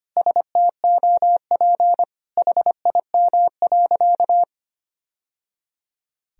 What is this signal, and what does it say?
Morse code, heard as a keyed tone, about 25 wpm, 700 Hz. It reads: STOP HIM.
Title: Sentences from Top 1000 Words - Review Part 4 - 25wpm